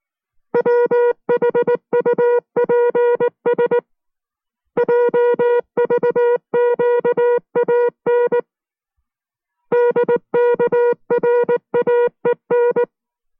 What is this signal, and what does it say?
Morse code, heard as a keyed tone, about 19 wpm, 480 Hz. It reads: WHUPS JVQAN DKRAEN